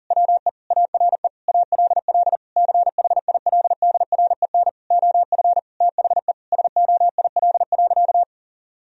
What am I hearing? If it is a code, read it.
Morse code, heard as a keyed tone, about 40 wpm, 700 Hz. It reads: WE ARE ALL CHILDREN OF THE SOIL.
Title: Sentences from Top 1000 Words - Review Part 4 - 40wpm